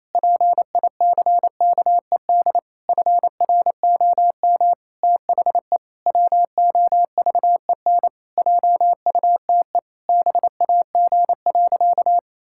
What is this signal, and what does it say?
Morse code, heard as a keyed tone, about 28 wpm, 700 Hz. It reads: PICKED FROM THE WOVEN JUTE BAG.
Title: Pangram Challenge v2 - 28wpm